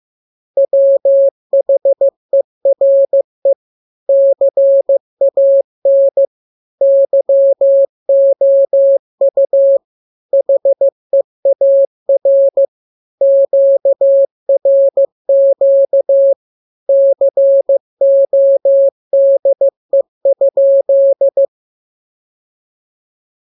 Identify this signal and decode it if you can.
Morse code, heard as a keyed tone, about 15 wpm, 550 Hz. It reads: WHERE CAN YOU HEAR QRQ CODE?